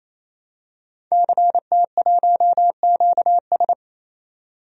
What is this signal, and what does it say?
Morse code, heard as a keyed tone, about 28 wpm, 700 Hz. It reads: CT1QS
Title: DXCC-100 Call Signs - 5 Characters - 28wpm